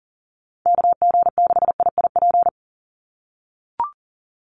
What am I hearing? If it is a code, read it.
Morse code, heard as a keyed tone, about 40 wpm, 700 Hz. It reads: KG6IIP
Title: Call Signs - 40wpm